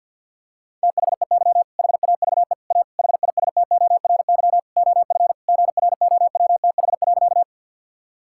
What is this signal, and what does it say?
Morse code, heard as a keyed tone, about 52 wpm, 690 Hz. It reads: THEY HAVE A HISTORY OF GROWTH.
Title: Sentences from Top 1000 Words - Review Part 5 - 50wpm